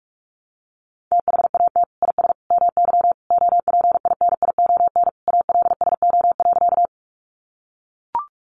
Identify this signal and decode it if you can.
Morse code, heard as a keyed tone, about 44 wpm, 700 Hz. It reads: THAT IS MY OPINION ALSO.
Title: Sentences from Top 1200 Words 45wpm